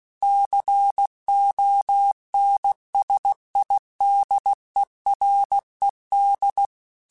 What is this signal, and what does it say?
Morse code, heard as a keyed tone, about 16 wpm, 800 Hz. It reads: CONSIDERED
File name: Technical Subjects / Technical Subjects-mp3s / XL @15